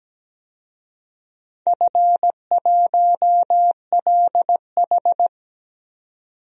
Morse code, heard as a keyed tone, about 17 wpm, 700 Hz. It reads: F1LH
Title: DXCC-100 Call Signs - 4 Characters - 17wpm